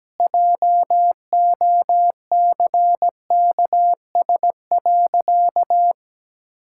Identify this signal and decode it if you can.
Morse code, heard as a keyed tone, about 17 wpm, 700 Hz. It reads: JOCKS.
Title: Pangram Challenge v2 - 17wpm